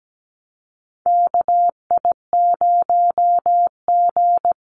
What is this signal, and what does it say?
Morse code, heard as a keyed tone, about 17 wpm, 700 Hz. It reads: KI0G
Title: NA SOTA Chaser Callsigns 17wpm